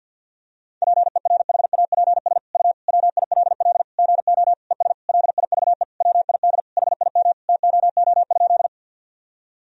Morse code, heard as a keyed tone, about 52 wpm, 700 Hz. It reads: PERHAPS U WILL GO ES LIVE WID HIM TOO?
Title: Sentences from Top 500 Words and Abbreviations 50wpm